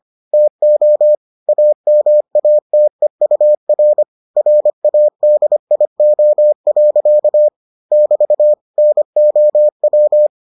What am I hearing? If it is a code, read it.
Morse code, heard as a keyed tone, about 25 wpm, 600 Hz. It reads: TO AMATEUR RADIO. <BT> NOW